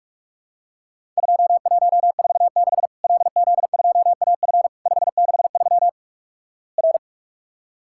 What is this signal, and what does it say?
Morse code, heard as a keyed tone, about 46 wpm, 700 Hz. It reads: 2146 LZ2AF 563 R
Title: Worked All Europe DX Contest - QTC Exchange - 45wpm